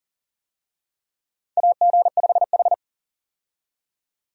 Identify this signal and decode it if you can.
Morse code, heard as a keyed tone, about 40 wpm, 700 Hz. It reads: AG5H
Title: USA Call Signs - 4 Characters - 40wpm